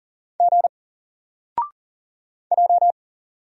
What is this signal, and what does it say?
Morse code, heard as a keyed tone, about 40 wpm, 700 Hz. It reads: G J
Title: Single Character (Letter:Number:Punctuation:Prosign) 40wpm